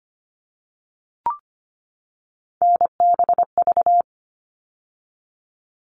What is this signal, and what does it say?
Morse code, heard as a keyed tone, about 25 wpm, 700 Hz. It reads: NBV
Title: All Cut Numbers - 3-Digit 25wpm